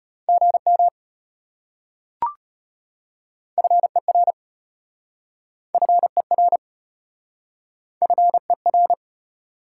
Morse code, heard as a keyed tone, about 33 wpm, 700 Hz. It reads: GM FER FER FER